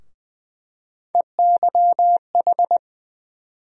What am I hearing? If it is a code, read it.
Morse code, heard as a keyed tone, about 20 wpm, 700 Hz. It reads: EYH
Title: USA Call Sign Postfixes - 3 Letters 20wpm